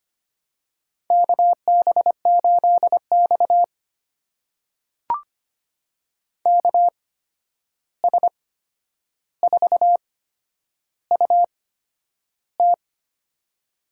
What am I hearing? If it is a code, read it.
Morse code, heard as a keyed tone, about 25 wpm, spaced as for 10 wpm, 700 Hz. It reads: KB8X KS4UT
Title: USA Call Signs - 8x Character Spacing - 25wpm